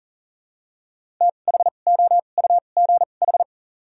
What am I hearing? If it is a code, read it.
Morse code, heard as a keyed tone, about 40 wpm, spaced as for 29 wpm, 700 Hz. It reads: THOUGH
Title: 2x Character Spacing - Top 500 Words - 40wpm